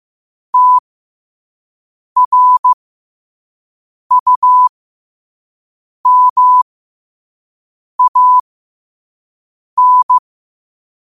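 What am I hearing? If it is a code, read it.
Morse code, heard as a keyed tone, about 15 wpm, spaced as for 5 wpm, 1000 Hz. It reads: TRUMAN
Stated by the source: Morse code sent at 5 wpm for learning.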